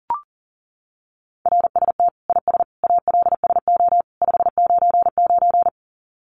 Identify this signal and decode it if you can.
Morse code, heard as a keyed tone, about 37 wpm, 700 Hz. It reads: RST IS ALSO 599
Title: Common QSO Phrases 40wpm